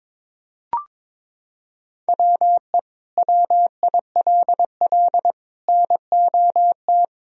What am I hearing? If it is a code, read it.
Morse code, heard as a keyed tone, about 21 wpm, 700 Hz. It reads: WE WILL NOT
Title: Sentences from Top 1000 Words - Review Part 2 - 22wpm